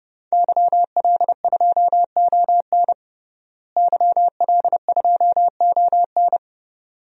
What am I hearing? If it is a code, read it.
Morse code, heard as a keyed tone, about 30 wpm, 700 Hz. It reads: YL2ON YL2ON
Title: DXCC-100 Call Signs - 5 Characters - Repeated 2x - 30wpm